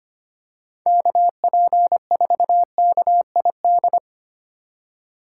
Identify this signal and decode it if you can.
Morse code, heard as a keyed tone, about 29 wpm, 700 Hz. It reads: KP4KID
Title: Speed Racing - USA 2x3 Call Signs 25wpm